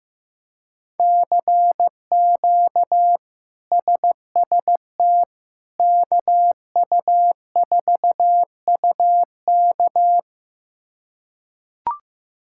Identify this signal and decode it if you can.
Morse code, heard as a keyed tone, about 15 wpm, 700 Hz. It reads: CQ SST KU4UK
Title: K1USN SST QSO Practice - 15wpm